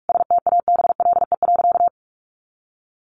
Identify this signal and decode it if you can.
Morse code, heard as a keyed tone, about 44 wpm, 700 Hz. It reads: STABLE.
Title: Sentences from Top 1500 Words 45wpm